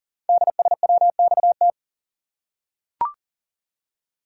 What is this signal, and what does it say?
Morse code, heard as a keyed tone, about 40 wpm, 700 Hz. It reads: DSWXT
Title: Code Groups - 5 Letters - 40wpm